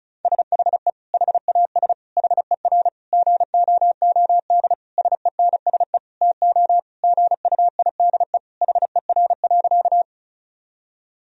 Morse code, heard as a keyed tone, about 34 wpm, 700 Hz. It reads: SHE HAS HER GOOD SENSE TO GUIDE HER.
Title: Sentences from Top 1000 Words - Review Part 4 - 35wpm